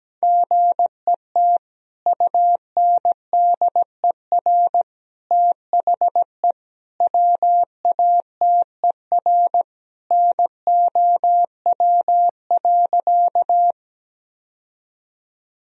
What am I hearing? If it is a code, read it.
Morse code, heard as a keyed tone, about 17 wpm, 700 Hz. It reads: GET UNDER THE WATER NOW.
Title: Sentences from Top 1000 Words - Review Part 1 - 17wpm